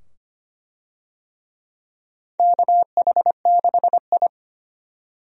Morse code, heard as a keyed tone, about 25 wpm, 700 Hz. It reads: KH6I
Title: DXCC-100 Call Signs - 4 Characters - 25wpm